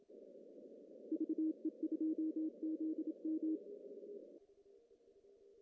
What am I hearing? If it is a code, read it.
Morse code, heard as a keyed tone, about 28 wpm, 320 Hz. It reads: VE2ZM
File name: CW / cqww-cw-k5tr